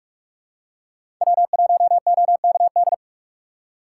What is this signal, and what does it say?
Morse code, heard as a keyed tone, about 46 wpm, 700 Hz. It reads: W1OKD
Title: USA Call Signs - 5 Characters - 4x Character Spacing - 45wpm